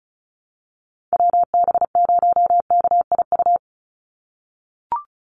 Morse code, heard as a keyed tone, about 34 wpm, 700 Hz. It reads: WB0KIU
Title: NA SOTA Chaser Callsigns 35wpm